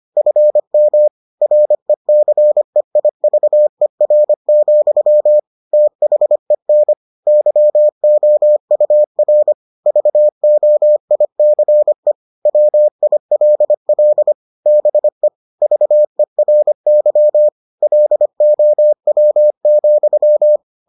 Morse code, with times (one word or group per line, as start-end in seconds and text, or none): 0.2-1.1 FM
1.4-5.4 RECEIVER,
5.7-6.9 THEN
7.3-9.5 YOUR
9.9-12.1 VOICE
12.5-14.3 WILL
14.7-15.3 BE
15.6-17.5 VERY
17.8-20.6 LOW,